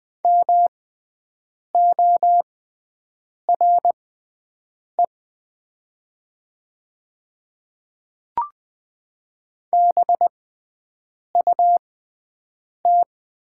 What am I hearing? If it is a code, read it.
Morse code, heard as a keyed tone, about 20 wpm, spaced as for 7 wpm, 700 Hz. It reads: MORE BUT
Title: 6x Character Spacing - Top 200 Words - 20wpm